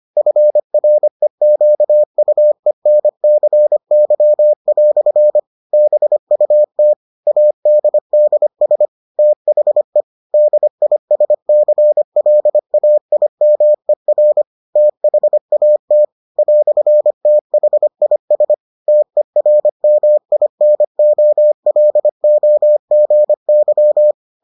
Morse code, heard as a keyed tone, about 25 wpm, 600 Hz. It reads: FREQUENCY" BUT ADDS THE DISCLAIMER THAT "THIS TERMINOLOGY